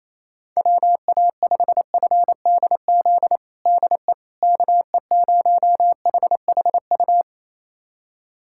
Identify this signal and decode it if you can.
Morse code, heard as a keyed tone, about 28 wpm, 700 Hz. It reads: WA5FDZ DE KE0HHU
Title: USA Callsign DE Callsign 28wpm